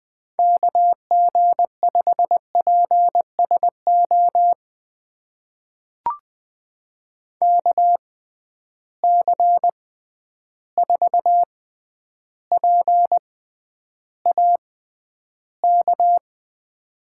Morse code, heard as a keyed tone, about 20 wpm, spaced as for 10 wpm, 700 Hz. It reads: KG5PSO KC4PAK
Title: USA Call Signs - 6 Characters - 6x Character Spacing - 20wpm